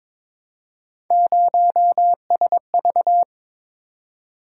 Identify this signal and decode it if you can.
Morse code, heard as a keyed tone, about 22 wpm, 700 Hz. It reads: 0SV
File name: practice / 22wpm / Three Letter-Number Combinations - 22wpm